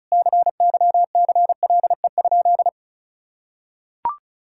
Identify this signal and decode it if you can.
Morse code, heard as a keyed tone, about 36 wpm, 700 Hz. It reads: CYCLE?
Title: Sentences from Top 1000 Words - Review Part 6 - 35wpm